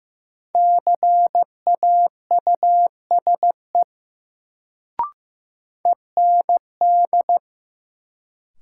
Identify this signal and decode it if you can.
Morse code, heard as a keyed tone, about 15 wpm, 700 Hz. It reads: CAUSE END
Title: Sentences from Top 200 Words - New Word Drill - 15wpm